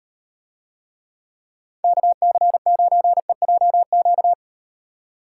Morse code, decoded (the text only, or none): KC9EJQ